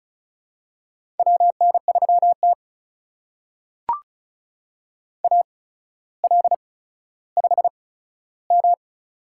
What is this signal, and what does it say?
Morse code, heard as a keyed tone, about 36 wpm, spaced as for 15 wpm, 700 Hz. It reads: WN3T AL5M
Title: USA Call Signs - 4 Characters - 8x Character Spacing - 35wpm